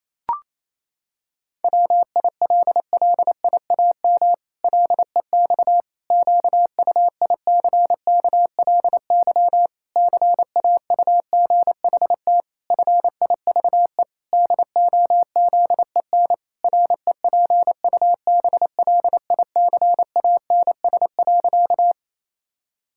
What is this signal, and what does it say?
Morse code, heard as a keyed tone, about 28 wpm, 700 Hz. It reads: WILLIAM LEX QUICKLY CAUGHT FIVE DOZEN REPUBLICANS.
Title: Pangram Challenge v2 - 28wpm